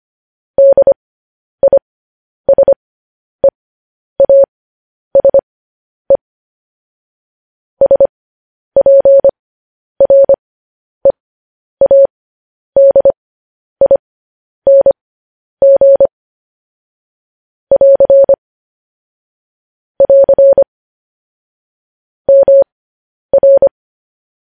Morse code, heard as a keyed tone, about 25 wpm, spaced as for 10 wpm, 550 Hz. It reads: DISEASE SPREADING <AR> <AR> MR
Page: News Headlines in Morse Code at 10 WPM